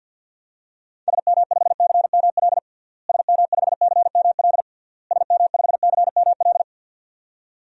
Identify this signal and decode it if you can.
Morse code, heard as a keyed tone, about 46 wpm, 700 Hz. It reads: SM5CML SM5CML SM5CML
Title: DXCC-100 Call Signs - 6 Characters - Repeated 3x - 50wpm